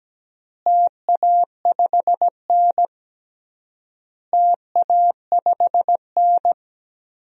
Morse code, heard as a keyed tone, about 17 wpm, 700 Hz. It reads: TA5N TA5N
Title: DXCC-100 Call Signs - 4 Characters - Repeated 3x - 17wpm